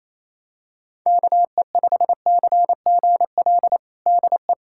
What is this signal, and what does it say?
Morse code, heard as a keyed tone, about 28 wpm, 700 Hz. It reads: KE5CGL DE
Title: USA Callsign DE Callsign - 6 Characters - 28wpm